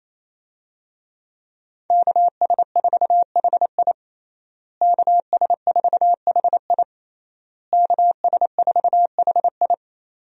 Morse code, heard as a keyed tone, about 28 wpm, 700 Hz. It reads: KS4HI KS4HI KS4HI
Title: USA Call Signs - 5 Characters - Repeated 3x - 28wpm